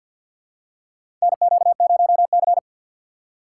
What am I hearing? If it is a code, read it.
Morse code, heard as a keyed tone, about 48 wpm, 700 Hz. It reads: NQ0C
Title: USA Call Signs - 2x Character Spacing - 50wpm